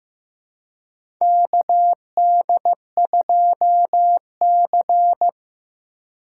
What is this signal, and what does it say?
Morse code, heard as a keyed tone, about 15 wpm, 700 Hz. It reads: KD2C
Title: USA Call Signs - 3x Character Spacing - 15wpm